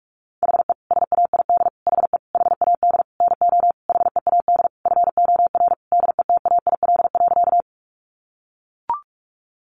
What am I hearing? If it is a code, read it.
Morse code, heard as a keyed tone, about 44 wpm, 700 Hz. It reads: HE SAID HE HAD NO HEAD FOR DETAIL.
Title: Sentences from Top 1500 Words 45wpm